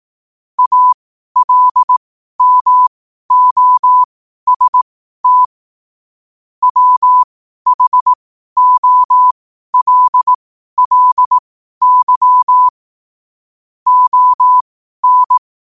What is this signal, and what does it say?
Morse code, heard as a keyed tone, about 18 wpm, spaced as for 12 wpm, 1000 Hz. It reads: ALMOST WHOLLY ON
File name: qotdfiles / 12wpm